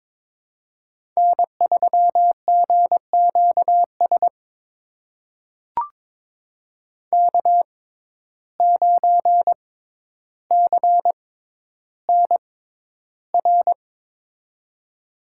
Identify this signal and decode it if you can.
Morse code, heard as a keyed tone, about 22 wpm, spaced as for 11 wpm, 700 Hz. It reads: N3GQS K9CNR
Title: USA Call Signs - 5 Characters - 6x Character Spacing - 22wpm